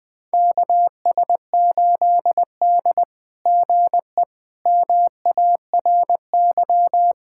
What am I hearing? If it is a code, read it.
Morse code, heard as a keyed tone, about 20 wpm, 700 Hz. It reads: KS8D GE MARY
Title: SOTA QSO Practice 20wpm